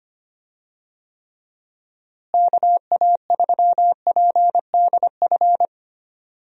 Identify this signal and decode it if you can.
Morse code, heard as a keyed tone, about 25 wpm, 700 Hz. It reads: KA3PDF
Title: USA Call Signs - 6 Characters - Repeated 3x - 25wpm